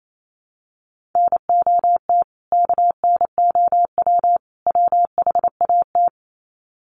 Morse code, heard as a keyed tone, about 28 wpm, 700 Hz. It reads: NOT KNOW WHAT
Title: Sets of 3 Words 28wpm